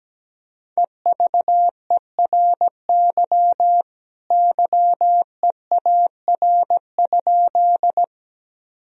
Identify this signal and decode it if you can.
Morse code, heard as a keyed tone, about 17 wpm, 700 Hz. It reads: EVERY YEAR?
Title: Sentences from Top 1000 Words - Review Part 1 - 17wpm